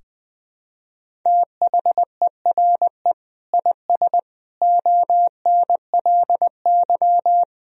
Morse code, heard as a keyed tone, about 20 wpm, 700 Hz. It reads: THERE IS ONLY
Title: Sentences from Top 300 Words and Abbreviations 20wpm